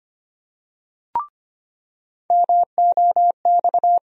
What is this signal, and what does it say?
Morse code, heard as a keyed tone, about 25 wpm, 700 Hz. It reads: MOX
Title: USA Call Sign Postfixes - 3 Letters 25wpm